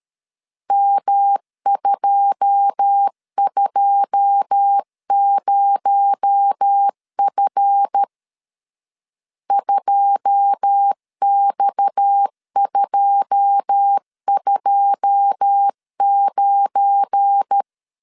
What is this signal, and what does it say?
Morse code, heard as a keyed tone, about 13 wpm, 800 Hz. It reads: M220F 2X229
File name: LEZIONI / 9 2 12wpm 2301